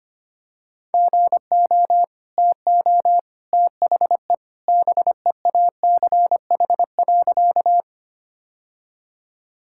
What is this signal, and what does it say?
Morse code, heard as a keyed tone, about 25 wpm, 690 Hz. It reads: GO TO THE BEACH.
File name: Easy Sentences with Hints - 25wpm